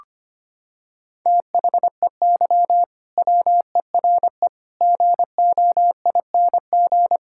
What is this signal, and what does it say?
Morse code, heard as a keyed tone, about 25 wpm, 700 Hz. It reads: THEY WERE GOING
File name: practice / 25wpm / Sentences with 3-Word Phrases - 25wpm